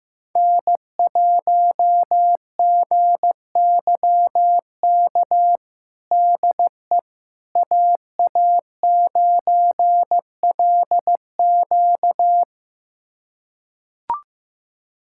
Morse code, heard as a keyed tone, about 15 wpm, 700 Hz. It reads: N1GYK DE AA9LQ
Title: USA Callsign DE Callsign - 5 Characters - 15wpm